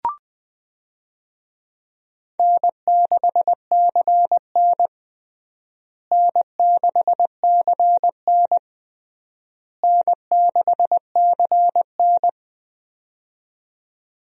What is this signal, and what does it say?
Morse code, heard as a keyed tone, about 20 wpm, 700 Hz. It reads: N6CN N6CN N6CN
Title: USA Call Signs - 4 Characters - Repeated 3x - 20wpm